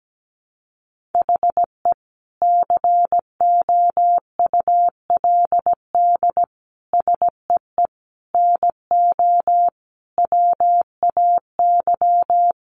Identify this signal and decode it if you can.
Morse code, heard as a keyed tone, about 17 wpm, 700 Hz. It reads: HE COULD SEE NO WAY
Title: Sentences from Top 1200 Words 17wpm